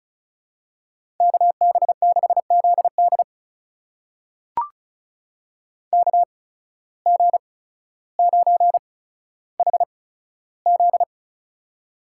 Morse code, heard as a keyed tone, about 36 wpm, spaced as for 15 wpm, 700 Hz. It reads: KB6ZD KG9HZ
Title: USA Call Signs - 5 Characters - 8x Character Spacing - 35wpm